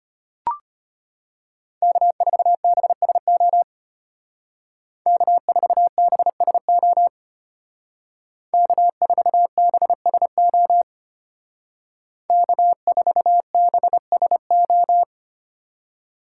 Morse code, heard as a keyed tone, about 31 wpm, 700 Hz. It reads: K4BSO K4BSO K4BSO K4BSO
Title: Speed Racing - USA 1x3 Call Signs 25wpm